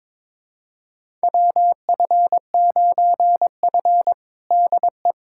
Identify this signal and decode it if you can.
Morse code, heard as a keyed tone, about 22 wpm, 700 Hz. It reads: WF9F DE
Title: USA Callsign DE Callsign - 4 Characters - 22wpm